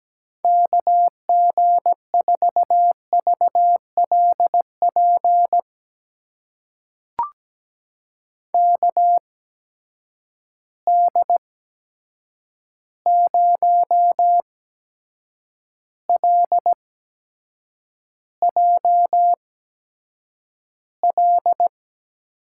Morse code, heard as a keyed tone, about 17 wpm, spaced as for 7 wpm, 700 Hz. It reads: KG4VLP KD0LJL
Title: USA Call Signs - 6 Characters - 8x Character Spacing - 17wpm